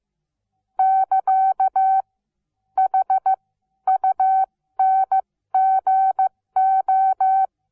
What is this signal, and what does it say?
Morse code, heard as a keyed tone, about 15 wpm, spaced as for 12 wpm, 760 Hz. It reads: <KA> HUNGO
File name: Onlinelehrgang / morsen / DJ4UF Morsen17